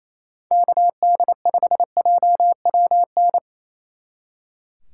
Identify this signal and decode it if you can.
Morse code, heard as a keyed tone, about 35 wpm, 700 Hz. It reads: KD5JWN
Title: Speed Racing - USA 2x3 Call Signs 28wpm